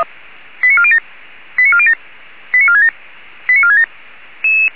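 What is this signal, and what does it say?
Five-tone selective call (ZVEI), heard as quick runs of five tones, tones of about 70 ms. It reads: 89407 89407 89467 89467
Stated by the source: ZVEI five-tone selective call received by SDR.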